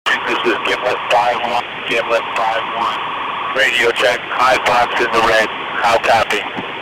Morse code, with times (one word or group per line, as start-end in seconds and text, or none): none